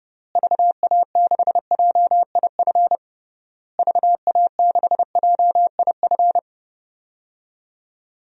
Morse code, heard as a keyed tone, about 30 wpm, 700 Hz. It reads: VA6JIF VA6JIF
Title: DXCC-100 Call Signs - 6 Characters - Repeated 3x - 30wpm